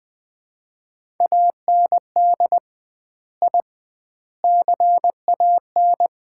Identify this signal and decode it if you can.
Morse code, heard as a keyed tone, about 20 wpm, 700 Hz. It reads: AND I CAN
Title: 2x Word-Spacing - Sets of 4 Words - 20wpm